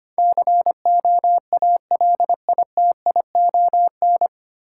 Morse code, heard as a keyed tone, about 25 wpm, 700 Hz. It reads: COALITION
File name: Nine Letter Words - 25wpm